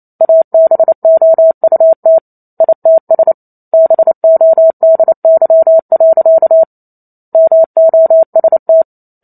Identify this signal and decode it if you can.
Morse code, heard as a keyed tone, about 28 wpm, 650 Hz. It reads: ABOUT ITS BODY. MOST